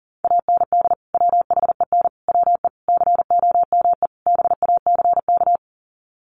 Morse code, heard as a keyed tone, about 40 wpm, 700 Hz. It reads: AND WHEN WE COME BACK
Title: Sets of 5 Words 40wpm